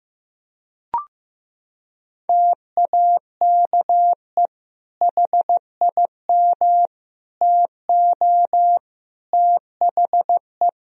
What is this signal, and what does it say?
Morse code, heard as a keyed tone, about 15 wpm, 700 Hz. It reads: TAKE HIM TO THE
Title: Sentences from Top 1000 Words - Review Part 4 - 15wpm